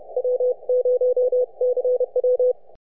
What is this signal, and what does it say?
Morse code, heard as a keyed tone, about 31 wpm, 500 Hz. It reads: W0CW
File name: n6tr